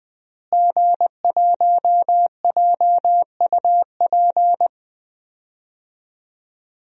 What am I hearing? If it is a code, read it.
Morse code, heard as a keyed tone, about 20 wpm, 700 Hz. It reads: G1JUP
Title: DXCC-100 Call Signs - 5 Characters - Repeated 3x - 20wpm